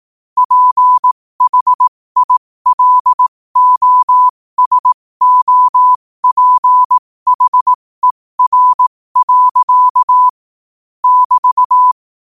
Morse code, heard as a keyed tone, about 18 wpm, spaced as for 15 wpm, 1000 Hz. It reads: PHILOSOPHER. <BT>
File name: qotdfiles / 15wpm